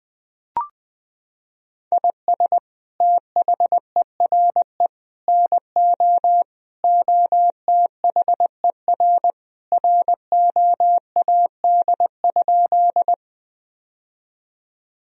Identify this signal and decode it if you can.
Morse code, heard as a keyed tone, about 20 wpm, 700 Hz. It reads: IS THERE NO OTHER ROAD?
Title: Sentences from Top 1000 Words - Review Part 2 - 20wpm